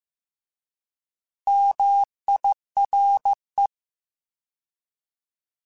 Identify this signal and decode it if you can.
Morse code, heard as a keyed tone, about 15 wpm, 800 Hz. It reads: MIRE